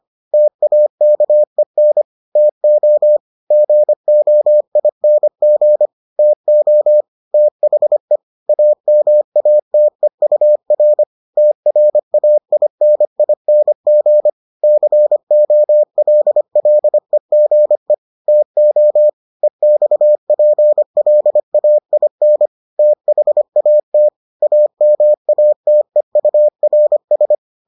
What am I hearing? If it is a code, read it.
Morse code, heard as a keyed tone, about 25 wpm, 600 Hz. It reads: TAKEN TO GOING TO THE AMATEUR TRAINING COLLEGE TO EXPLAIN THAT AMATEURS